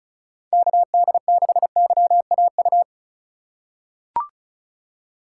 Morse code, heard as a keyed tone, about 36 wpm, 700 Hz. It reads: KD6YAU
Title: USA Call Signs - 6 Characters - 4x Character Spacing - 35wpm